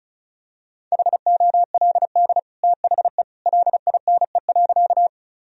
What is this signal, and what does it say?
Morse code, heard as a keyed tone, about 34 wpm, 700 Hz. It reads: HOLD THE LINE.
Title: Sentences from Top 1000 Words - Review Part 2 - 35wpm